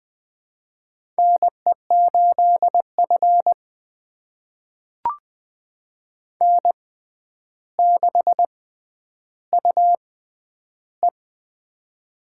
Morse code, heard as a keyed tone, about 20 wpm, spaced as for 10 wpm, 700 Hz. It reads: NE8F N6UE
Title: USA Call Signs - 4 Characters - 6x Character Spacing - 20wpm